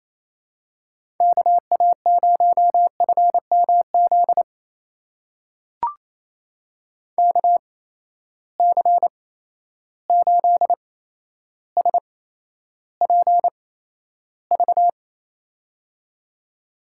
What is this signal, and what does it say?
Morse code, heard as a keyed tone, about 28 wpm, spaced as for 12 wpm, 700 Hz. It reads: KA0FMZ KC8SPV
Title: USA Call Signs - 6 Characters - 8x Character Spacing - 28wpm